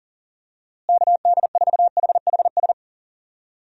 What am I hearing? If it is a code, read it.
Morse code, heard as a keyed tone, about 37 wpm, 700 Hz. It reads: KD4HHS
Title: USA Call Signs - 6 Characters - Repeated 2x - 40wpm